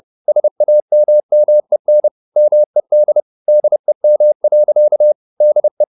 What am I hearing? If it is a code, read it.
Morse code, heard as a keyed tone, about 30 wpm, 600 Hz. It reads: SAMMEN MED DEM. DE
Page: Ferie 30 wpm